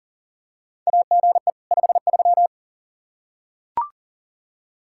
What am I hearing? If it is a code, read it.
Morse code, heard as a keyed tone, about 40 wpm, 700 Hz. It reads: AGE 53
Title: Practice QSO - Exchange 3 - More Info - 40wpm